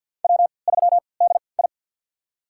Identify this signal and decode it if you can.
Morse code, heard as a keyed tone, about 53 wpm, spaced as for 29 wpm, 700 Hz. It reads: W3DI
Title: USA Call Signs - 4 Characters - 3x Character Spacing - 50wpm